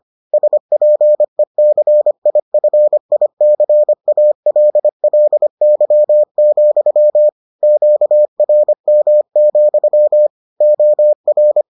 Morse code, heard as a keyed tone, about 25 wpm, 600 Hz. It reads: SPECIFICALLY, QRM, OR